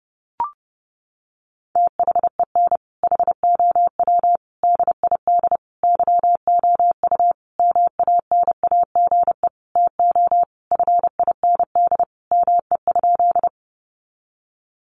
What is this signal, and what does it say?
Morse code, heard as a keyed tone, about 30 wpm, 700 Hz. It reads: THEN HOW DID YOU MANAGE TO FIND ME?
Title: Sentences from Top 1400 Words 30wpm